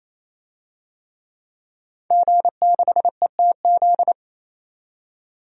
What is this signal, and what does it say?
Morse code, heard as a keyed tone, about 28 wpm, 700 Hz. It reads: G6ETZ
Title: DXCC-100 Call Signs - 5 Characters - 28wpm